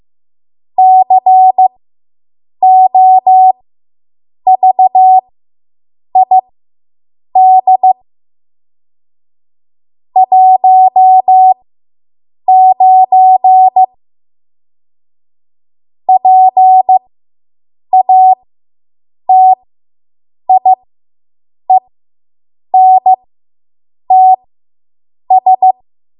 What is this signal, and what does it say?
Morse code, heard as a keyed tone, about 15 wpm, spaced as for 7 wpm, 750 Hz. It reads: COVID 19 PATIENTS